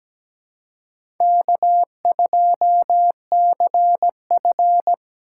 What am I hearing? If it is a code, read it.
Morse code, heard as a keyed tone, about 17 wpm, 700 Hz. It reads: K2CF
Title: USA Call Signs - 4 Characters - 4x Character Spacing - 17wpm